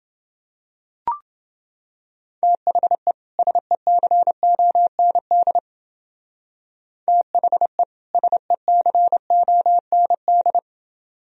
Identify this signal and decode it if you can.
Morse code, heard as a keyed tone, about 29 wpm, 700 Hz. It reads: THE SECOND THE SECOND